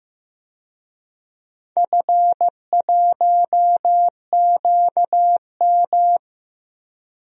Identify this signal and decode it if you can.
Morse code, heard as a keyed tone, about 15 wpm, 700 Hz. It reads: F1QM
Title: DXCC-100 Call Signs - 4 Characters - Repeated 3x - 15wpm